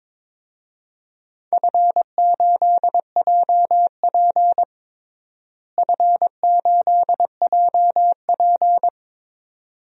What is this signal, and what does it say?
Morse code, heard as a keyed tone, about 22 wpm, 700 Hz. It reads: F8JP F8JP
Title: DXCC-100 Call Signs - 4 Characters - Repeated 2x - 22wpm